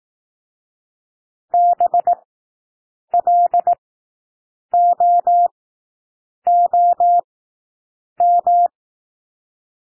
Morse code, heard as a keyed tone, about 18 wpm, spaced as for 7 wpm, 700 Hz. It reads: BLOOM
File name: SOUND WORDS 18 wpm